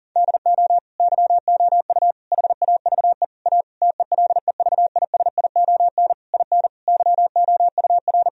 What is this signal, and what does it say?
Morse code, heard as a keyed tone, about 40 wpm, 700 Hz. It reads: DO YOU HAVE A TELEVISION IN YOUR
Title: Sentences from Top 1000 Words - Review Part 4 - 40wpm